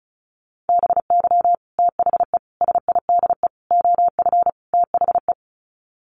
Morse code, heard as a keyed tone, about 34 wpm, 700 Hz. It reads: BY THE SIDE OF THE
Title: Sets of 5 Words Encore2 35wpm